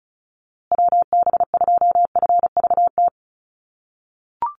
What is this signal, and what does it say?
Morse code, heard as a keyed tone, about 34 wpm, 700 Hz. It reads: WB2FVT
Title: USA Call Signs Repeated 3x 35wpm